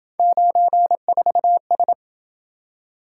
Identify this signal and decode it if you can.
Morse code, heard as a keyed tone, about 27 wpm, 700 Hz. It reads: 94S